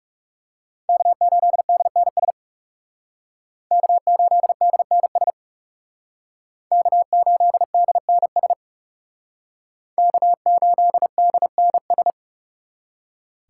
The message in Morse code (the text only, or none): K8DNS K8DNS K8DNS K8DNS